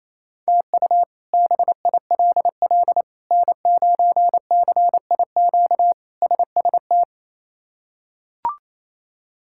Morse code, heard as a keyed tone, about 28 wpm, 700 Hz. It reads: TU BILL N9CIQ SST
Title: K1USN SST QSO Practice - 28wpm